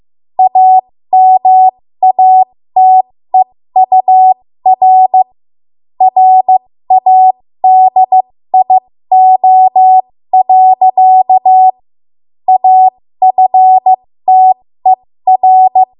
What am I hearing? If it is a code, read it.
Morse code, heard as a keyed tone, about 15 wpm, spaced as for 13 wpm, 750 Hz. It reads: AMATEUR RADIO. AFTER
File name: Morse / 13WPM